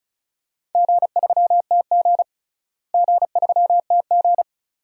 Morse code, heard as a keyed tone, about 36 wpm, 700 Hz. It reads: G3TG G3TG
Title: DXCC-100 Call Signs - 4 Characters - Repeated 2x - 35wpm